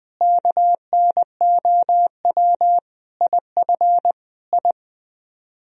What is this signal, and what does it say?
Morse code, heard as a keyed tone, about 20 wpm, 700 Hz. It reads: KNOW IF I
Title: Speed-Racing - Sets of 5 Words 17wpm